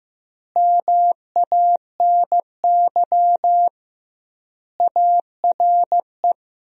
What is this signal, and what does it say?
Morse code, heard as a keyed tone, about 15 wpm, 700 Hz. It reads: MANY ARE
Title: 2x Word-Spacing - Sentences from Top 100 Words - 15wpm